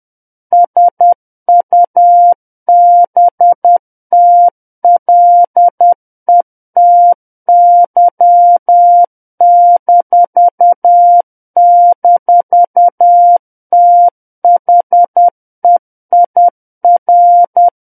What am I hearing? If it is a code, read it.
Morse code, heard as a keyed tone, about 10 wpm, 700 Hz. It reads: SUBTLETY--THEIR
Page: War of the Worlds - 01-Chapter 1 - 10 WPM